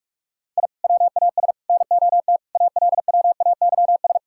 Morse code, heard as a keyed tone, about 46 wpm, 700 Hz. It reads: I WAS NOT ALWAYS
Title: Sentences from Top 1000 Words - Review Part 6 - 45wpm